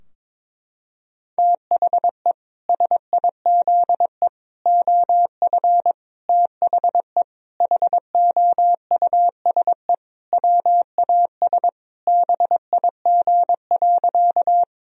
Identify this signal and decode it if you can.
Morse code, heard as a keyed tone, about 22 wpm, 700 Hz. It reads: THE SIZE OF THE HOUSE WAS BIG.
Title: Sentences from Top 1000 Words - Review Part 2 - 22wpm